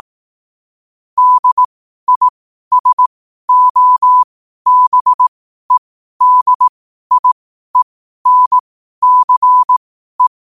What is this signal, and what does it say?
Morse code, heard as a keyed tone, about 18 wpm, spaced as for 12 wpm, 1000 Hz. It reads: DISOBEDIENCE
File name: qotdfiles / 12wpm